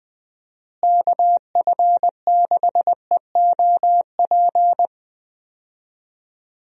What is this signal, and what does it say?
Morse code, heard as a keyed tone, about 21 wpm, 700 Hz. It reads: KF6EOP
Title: USA Call Signs - 6 Characters - Repeated 3x - 20wpm